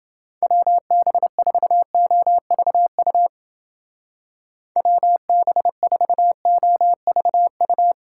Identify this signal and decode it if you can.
Morse code, heard as a keyed tone, about 29 wpm, 700 Hz. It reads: WB4OVU WB4OVU